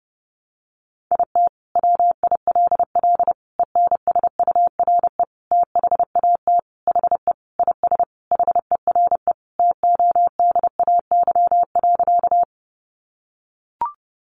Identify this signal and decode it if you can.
Morse code, heard as a keyed tone, about 30 wpm, 700 Hz. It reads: IT WILL ENSURE THAT HE IS HERE TODAY.
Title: Sentences from Top 1100 Words 30wpm